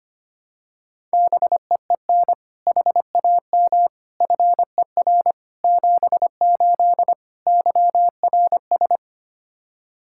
Morse code, heard as a keyed tone, about 25 wpm, 700 Hz. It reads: BEEN HAM FER 78 YRS
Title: Practice QSO - Exchange 3 - More Info - 25wpm